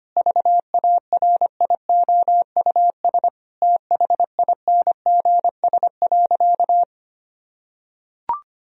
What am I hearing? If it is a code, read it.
Morse code, heard as a keyed tone, about 25 wpm, 700 Hz. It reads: VARIOUS THINGS.
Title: Sentences from Top 1000 Words - Review Part 4 - 25wpm